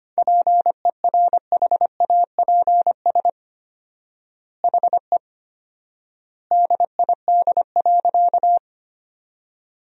Morse code, heard as a keyed tone, about 25 wpm, 700 Hz. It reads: PERHAPS HE DID.
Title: 4x Word-Spacing - Sentences from Top 500 Words - 25wpm